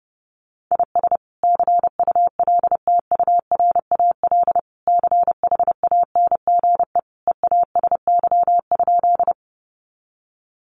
Morse code, heard as a keyed tone, about 30 wpm, 700 Hz. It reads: IS CULTURAL CHANGE EASY?
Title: Sentences from Top 1500 Words 30wpm